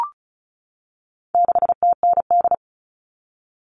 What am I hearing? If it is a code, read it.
Morse code, heard as a keyed tone, about 34 wpm, 700 Hz. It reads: 6TND
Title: All Cut Numbers - 4-Digit 35wpm